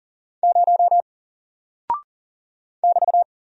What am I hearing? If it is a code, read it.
Morse code, heard as a keyed tone, about 40 wpm, 700 Hz. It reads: 0 <BT>
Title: Single Character (Letter:Number:Punctuation:Prosign) 40wpm